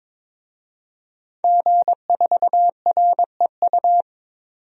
Morse code, heard as a keyed tone, about 21 wpm, 700 Hz. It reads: G4REU
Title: DXCC-100 Call Signs - 5 Characters - 22wpm